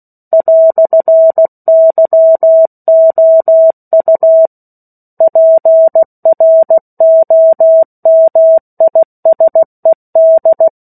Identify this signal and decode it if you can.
Morse code, heard as a keyed tone, about 16 wpm, 650 Hz. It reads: "YOU PROMISED
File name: 03-Chapter-03 16 WPM